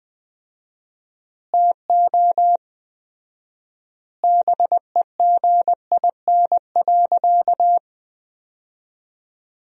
Morse code, heard as a keyed tone, about 20 wpm, 700 Hz. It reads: TO BEGIN.
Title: 4x Word-Spacing - Sentences from Top 300 Words - 20wpm